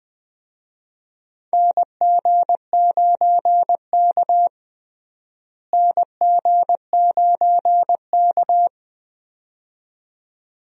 Morse code, heard as a keyed tone, about 21 wpm, 700 Hz. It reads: NG9K NG9K
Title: USA Call Signs - 4 Characters - Repeated 2x - 20wpm